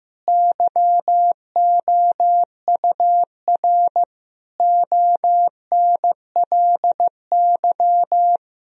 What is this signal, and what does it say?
Morse code, heard as a keyed tone, about 15 wpm, 700 Hz. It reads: YOUR ONLY
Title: Sentences from Top 1000 Words - Review Part 4 - 15wpm